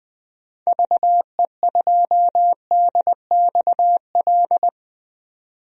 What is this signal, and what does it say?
Morse code, heard as a keyed tone, about 21 wpm, 700 Hz. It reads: VE2DXL
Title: DXCC-100 Call Signs - 6 Characters - 20wpm